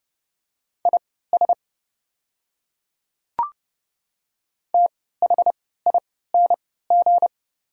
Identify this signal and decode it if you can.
Morse code, heard as a keyed tone, about 30 wpm, spaced as for 17 wpm, 700 Hz. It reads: IS THING
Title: 3x Character Spacing - Top 100 Words - 30wpm